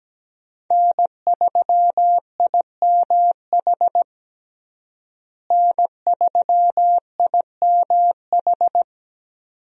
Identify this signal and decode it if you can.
Morse code, heard as a keyed tone, about 17 wpm, 700 Hz. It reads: N3IMH N3IMH
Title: USA Call Signs - 5 Characters - Repeated 2x - 17wpm